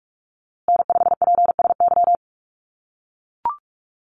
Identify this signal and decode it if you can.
Morse code, heard as a keyed tone, about 44 wpm, 700 Hz. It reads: N5PSY
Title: USA Call Signs Repeated 2x 45wpm